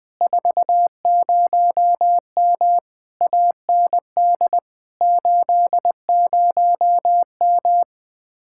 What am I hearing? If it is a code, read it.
Morse code, heard as a keyed tone, about 20 wpm, 700 Hz. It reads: 40M AND 80M
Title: Speed-Racing Common QSO Phrases - 17wpm